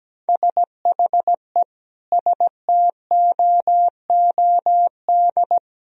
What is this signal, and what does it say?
Morse code, heard as a keyed tone, about 17 wpm, 700 Hz. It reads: SHE STOOD
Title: Sentences from Top 1000 Words - Review Part 3 - 17wpm